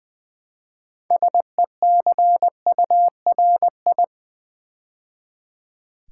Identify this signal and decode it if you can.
Morse code, heard as a keyed tone, about 20 wpm, 700 Hz. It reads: SECURI
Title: Build A Word - 3 Letters At A Time - 8-Letter Words - 20wpm